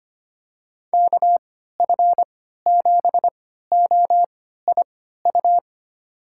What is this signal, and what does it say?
Morse code, heard as a keyed tone, about 25 wpm, spaced as for 14 wpm, 700 Hz. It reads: KF7OIU
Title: USA Call Signs - 6 Characters - 3x Character Spacing - 25wpm